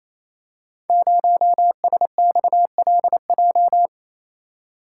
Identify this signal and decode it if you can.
Morse code, heard as a keyed tone, about 28 wpm, 700 Hz. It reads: 0SXLJ
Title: Code Groups - 5 Letter:Numbers - 28wpm